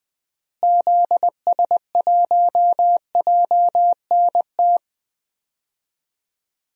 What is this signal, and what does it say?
Morse code, heard as a keyed tone, about 20 wpm, 700 Hz. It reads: ZS1JNT
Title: DXCC-100 Call Signs - 6 Characters - Repeated 3x - 20wpm